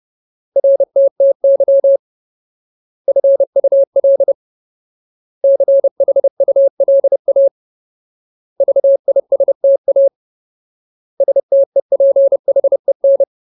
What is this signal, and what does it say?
Morse code, heard as a keyed tone, about 30 wpm, 550 Hz. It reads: RTTY FUL CHULA VISTA STEPHEN
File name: Ultimate Word List / 30WPM Ultimate